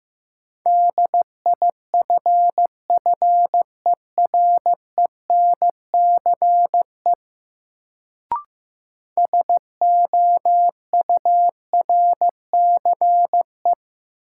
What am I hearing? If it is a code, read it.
Morse code, heard as a keyed tone, about 15 wpm, 700 Hz. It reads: DIFFERENCE SOURCE
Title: Sentences from Top 800 Words - New Word Drill - 15wpm